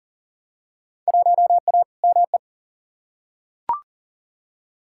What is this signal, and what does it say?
Morse code, heard as a keyed tone, about 38 wpm, 700 Hz. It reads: 1A ME
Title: Field Day Exchanges Encore - 40wpm